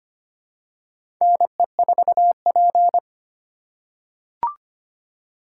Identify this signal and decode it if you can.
Morse code, heard as a keyed tone, about 25 wpm, 700 Hz. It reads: NE4P
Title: USA Call Signs - 4 Characters - 2x Character Spacing - 25wpm